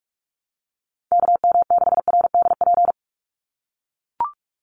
Morse code, heard as a keyed tone, about 44 wpm, 700 Hz. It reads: KM6RDP
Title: USA Call Signs 45wpm